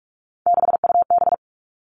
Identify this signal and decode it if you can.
Morse code, heard as a keyed tone, about 46 wpm, 700 Hz. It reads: 6UB
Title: Three Challenging Letter-Number Combinations 45wpm